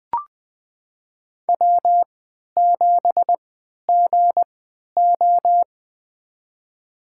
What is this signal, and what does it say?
Morse code, heard as a keyed tone, about 20 wpm, spaced as for 11 wpm, 700 Hz. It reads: W7GO
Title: USA Call Signs - 3x Character Spacing - 20wpm